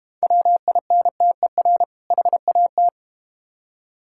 Morse code, heard as a keyed tone, about 32 wpm, 700 Hz. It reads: WINTER HAT